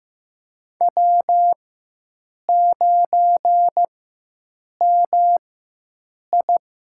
Morse code, heard as a keyed tone, about 15 wpm, spaced as for 7 wpm, 700 Hz. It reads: W9MI
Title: USA Call Signs - 4 Characters - 4x Character Spacing - 15wpm